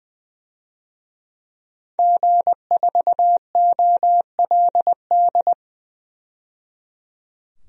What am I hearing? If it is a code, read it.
Morse code, heard as a keyed tone, about 20 wpm, 700 Hz. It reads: G4OLD
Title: DXCC-100 Call Signs - 5 Characters - 20wpm